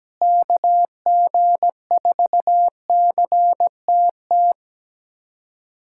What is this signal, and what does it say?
Morse code, heard as a keyed tone, about 17 wpm, 700 Hz. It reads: KG4CTT
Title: USA Callsign DE Callsign - 6 Characters - 17wpm